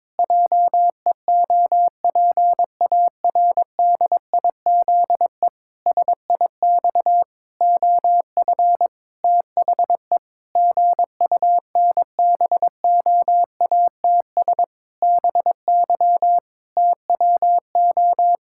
Morse code, heard as a keyed tone, about 22 wpm, 700 Hz. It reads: JEOPARDIZE SIX OF THE GUNBOATS BY TWO
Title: Pangram Challenge v2 - 22wpm